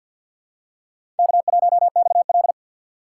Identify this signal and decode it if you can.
Morse code, heard as a keyed tone, about 51 wpm, 700 Hz. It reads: K1XL
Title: USA Call Signs - 4 Characters - 2x Character Spacing - 50wpm